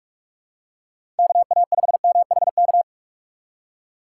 Morse code, heard as a keyed tone, about 46 wpm, 700 Hz. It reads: KA5MHK
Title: USA Call Signs - 6 Characters - 6x Character Spacing - 45wpm